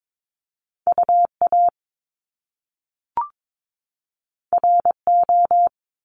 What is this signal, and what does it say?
Morse code, heard as a keyed tone, about 22 wpm, 700 Hz. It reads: UA RO
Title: Top Two Letter Combinations in Words 22wpm